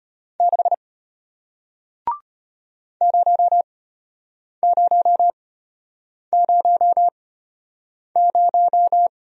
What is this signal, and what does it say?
Morse code, heard as a keyed tone, about 33 wpm, spaced as for 8 wpm, 700 Hz. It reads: 6 0000